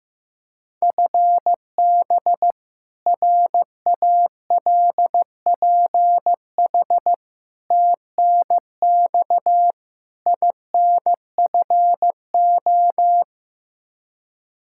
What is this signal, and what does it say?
Morse code, heard as a keyed tone, about 15 wpm, 700 Hz. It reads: FB RALPH TNX INFO
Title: Practice QSO - Exchange 3 - More Info - 15wpm